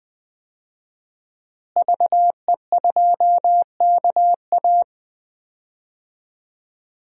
Morse code, heard as a keyed tone, about 20 wpm, 700 Hz. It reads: VE2KA
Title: DXCC-100 Call Signs - 5 Characters - Repeated 2x - 20wpm